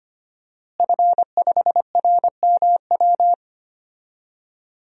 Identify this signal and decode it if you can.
Morse code, heard as a keyed tone, about 25 wpm, 700 Hz. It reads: F5RMW
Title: DXCC-100 Call Signs - 5 Characters - Repeated 3x - 25wpm